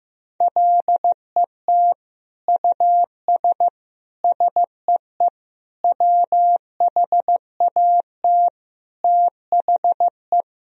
Morse code, heard as a keyed tone, about 16 wpm, 700 Hz. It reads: LET US SEE WHAT THE
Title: Sentences from Top 1000 Words - Review Part 1 - 15wpm